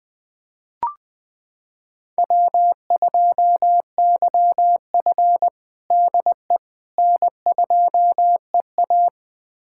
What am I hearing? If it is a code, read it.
Morse code, heard as a keyed tone, about 20 wpm, 700 Hz. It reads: W2YF DE N2EA
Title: USA Callsign DE Callsign - 4 Characters - 20wpm